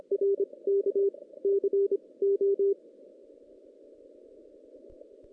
Morse code, heard as a keyed tone, about 27 wpm, 410 Hz. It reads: R K C O
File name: 2016 JAN CW NAQP / 2016 Jan naqp k5ot - 0-5